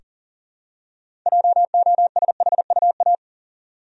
Morse code, heard as a keyed tone, about 40 wpm, 700 Hz. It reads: JOSHUA
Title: USA Common Names - 40wpm